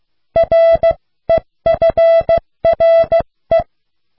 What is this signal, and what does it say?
Morse code, heard as a keyed tone, about 16 wpm, spaced as for 14 wpm, 660 Hz. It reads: REFRE